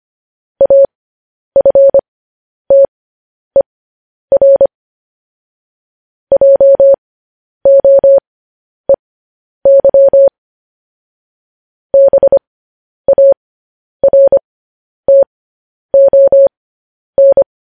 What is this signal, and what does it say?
Morse code, heard as a keyed tone, about 25 wpm, spaced as for 10 wpm, 550 Hz. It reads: AFTER JOEY BARTON
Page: news headlines in morse code at 10 wpm